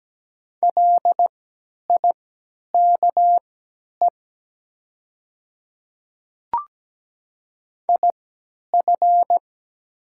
Morse code, heard as a keyed tone, about 17 wpm, spaced as for 10 wpm, 700 Hz. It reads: LIKE IF
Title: 3x Character Spacing - Top 100 Words - 17wpm